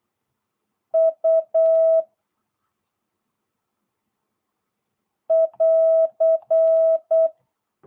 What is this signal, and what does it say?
Morse code, heard as a keyed tone, about 8 wpm, 660 Hz. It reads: U <AR>